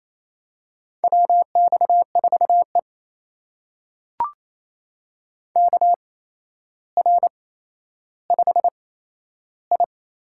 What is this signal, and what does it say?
Morse code, heard as a keyed tone, about 28 wpm, spaced as for 12 wpm, 700 Hz. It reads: WX4E KR5I
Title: USA Call Signs - 4 Characters - 8x Character Spacing - 28wpm